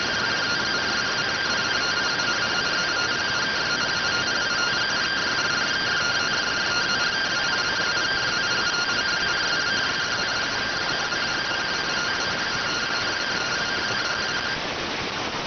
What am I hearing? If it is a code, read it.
Two-tone teleprinter signal, about 170 Hz shift.